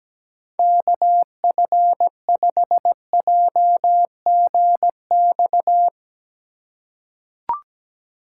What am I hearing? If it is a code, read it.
Morse code, heard as a keyed tone, about 17 wpm, 700 Hz. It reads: KF5JGX
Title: USA Call Signs - 6 Characters - 2x Character Spacing - 17wpm